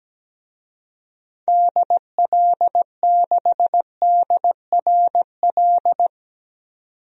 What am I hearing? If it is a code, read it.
Morse code, heard as a keyed tone, about 17 wpm, 700 Hz. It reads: DL6DRL
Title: DXCC-100 Call Signs - 6 Characters - 17wpm